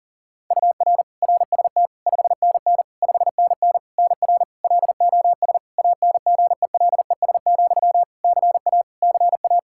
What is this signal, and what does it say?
Morse code, heard as a keyed tone, about 37 wpm, 700 Hz. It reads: UR RST 5NN 5NN NR LOS ANGELES, CA CA
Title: K1USN SST QSO Practice 40wpm